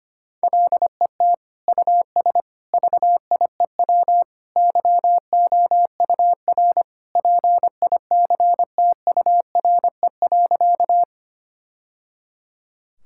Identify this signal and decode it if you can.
Morse code, heard as a keyed tone, about 25 wpm, 700 Hz. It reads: LET US VIEW YOUR PICTURE.
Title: Sentences from Top 1000 Words - Review Part 6 - 25wpm